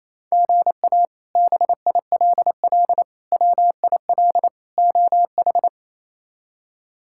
Morse code, heard as a keyed tone, about 28 wpm, 700 Hz. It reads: GA BILL WIL OH
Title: K1USN SST QSO Practice - 28wpm